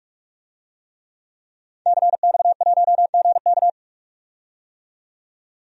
Morse code, heard as a keyed tone, about 44 wpm, 700 Hz. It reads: CX1GK
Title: DXCC-100 Call Signs - 5 Characters - 45wpm